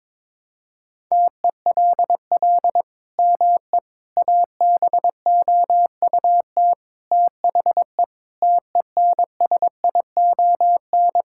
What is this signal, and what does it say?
Morse code, heard as a keyed tone, about 22 wpm, 700 Hz. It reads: TELL ME ABOUT THE TENSION
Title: Sentences from Top 1600 Words 22wpm